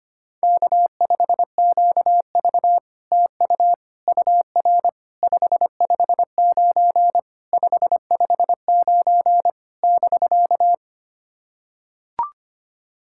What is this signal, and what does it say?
Morse code, heard as a keyed tone, about 25 wpm, 700 Hz. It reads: K5QV TU UR 559 559 <BK>
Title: SOTA QSO Practice 25wpm